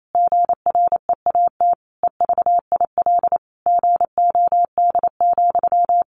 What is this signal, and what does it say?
Morse code, heard as a keyed tone, about 28 wpm, 700 Hz. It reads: GREAT EVIL GOD,
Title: A Sound Of Thunder 28wpm